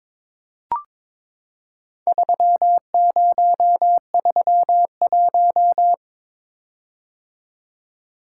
Morse code, heard as a keyed tone, about 22 wpm, 700 Hz. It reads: 3031
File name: practice / 22wpm / Four Letter-Number Combinations - 22wpm